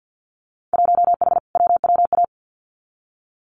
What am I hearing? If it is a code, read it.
Morse code, heard as a keyed tone, about 50 wpm, 700 Hz. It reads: 1H WWA
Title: Winter Field Day Exchanges Encore 50wpm